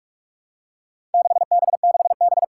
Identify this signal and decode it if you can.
Morse code, heard as a keyed tone, about 44 wpm, 700 Hz. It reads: 6B6B
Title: Four Challenging Letter-Number Combinations 45wpm